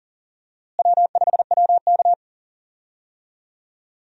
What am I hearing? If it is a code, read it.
Morse code, heard as a keyed tone, about 40 wpm, 700 Hz. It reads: W5WK
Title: USA Call Signs - 4 Characters - Repeated 2x - 40wpm